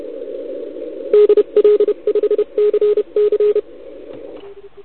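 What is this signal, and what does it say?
Morse code, heard as a keyed tone, about 31 wpm, 420 Hz, with radio noise. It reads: DL5CC
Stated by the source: CW from the OKOM DX contest on 80 m, received with a TS480.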